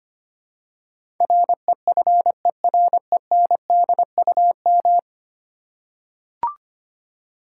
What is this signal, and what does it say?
Morse code, heard as a keyed tone, about 25 wpm, 700 Hz. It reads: REFERENDUM
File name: 10 Letter Words - First 3 Letters Given - 25wpm